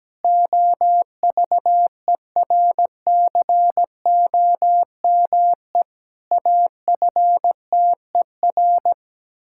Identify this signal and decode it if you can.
Morse code, heard as a keyed tone, about 17 wpm, 700 Hz. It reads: OVERCOME AFTER
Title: Pangram Challenge v2 - 17wpm